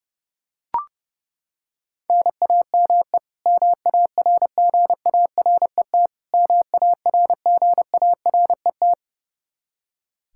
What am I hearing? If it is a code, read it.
Morse code, heard as a keyed tone, about 32 wpm, 700 Hz. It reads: NAME MARGARET MARGARET
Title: Practice QSO - Exchange 1 - Beginning - 30wpm